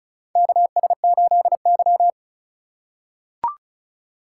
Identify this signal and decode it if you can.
Morse code, heard as a keyed tone, about 36 wpm, 700 Hz. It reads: KS8Y
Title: USA Call Signs - 4 Characters - 3x Character Spacing - 35wpm